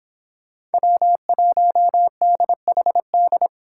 Morse code, heard as a keyed tone, about 31 wpm, 700 Hz. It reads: W1DHD